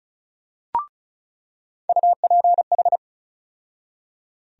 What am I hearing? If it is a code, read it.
Morse code, heard as a keyed tone, about 34 wpm, 700 Hz. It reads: UPH
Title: USA Call Sign Postfixes - 3 Letters 35wpm